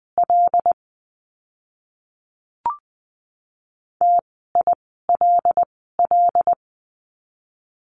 Morse code, heard as a keyed tone, about 20 wpm, spaced as for 15 wpm, 700 Hz. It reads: L TILL
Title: 075 Lesson 13f - Letter L - Words 20wpm